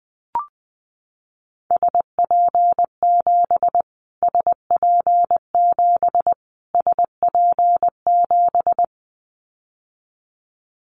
Morse code, heard as a keyed tone, about 20 wpm, 700 Hz. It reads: SP7 SP7 SP7
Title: Callsign Prefixes for Top 25 DXCC - Repeated 3x - 20wpm